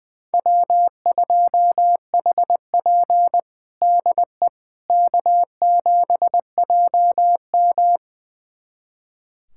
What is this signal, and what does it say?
Morse code, heard as a keyed tone, about 20 wpm, 700 Hz. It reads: W2HP DE K7JM
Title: USA Callsign DE Callsign - 4 Characters - 20wpm